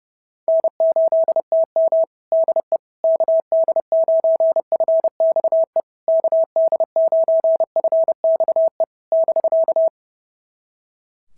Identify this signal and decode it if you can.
Morse code, heard as a keyed tone, about 28 wpm, 690 Hz. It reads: N8TM DE KD9FXE KD9FXE <BK>
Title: SOTA QSO Practice 30wpm